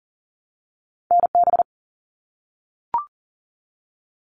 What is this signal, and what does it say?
Morse code, heard as a keyed tone, about 42 wpm, 700 Hz. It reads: NB
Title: Canadian Provinces and Territories 40wpm